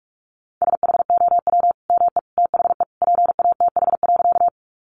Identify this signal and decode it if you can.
Morse code, heard as a keyed tone, about 44 wpm, 700 Hz. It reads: SHOW ME THE PATH.
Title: Sentences from Top 1400 Words 45wpm